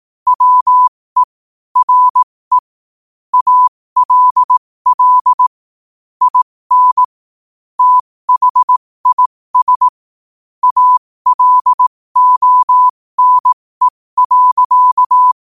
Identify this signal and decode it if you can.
Morse code, heard as a keyed tone, about 18 wpm, spaced as for 15 wpm, 1000 Hz. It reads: WERE ALL IN THIS ALONE.
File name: qotdfiles / 15wpm